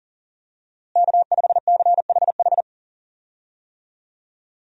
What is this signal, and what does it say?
Morse code, heard as a keyed tone, about 40 wpm, 700 Hz. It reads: K5CHH
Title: USA Call Signs - 5 Characters - Repeated 3x - 40wpm